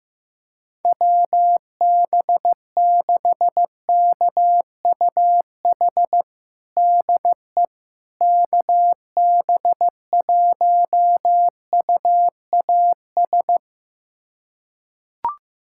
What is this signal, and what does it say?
Morse code, heard as a keyed tone, about 15 wpm, 700 Hz. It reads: WB6KUH DE KB1UAS
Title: USA Callsign DE Callsign - 6 Characters - 15wpm